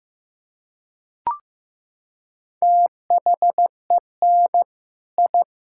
Morse code, heard as a keyed tone, about 16 wpm, 700 Hz. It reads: THEN I
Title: Sentences from Top 1000 Words - Review Part 6 - 15wpm